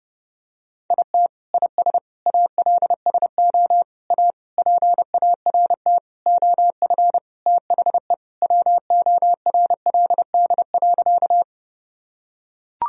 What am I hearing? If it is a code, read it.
Morse code, heard as a keyed tone, about 29 wpm, 700 Hz. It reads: IT IS ALSO A PART OF THE WORLD.
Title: Sentences from Top 1000 Words - Review Part 1 - 30wpm